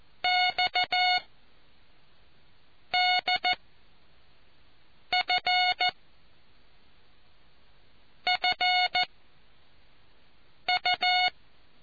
Morse code, heard as a keyed tone, about 14 wpm, 730 Hz. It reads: X D F F U